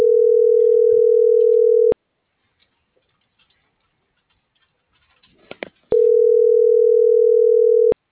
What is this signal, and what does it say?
Ringback tone (North America), 440 + 480 Hz.
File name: CENTRAL SKYBLUE / out lorena.gonzalez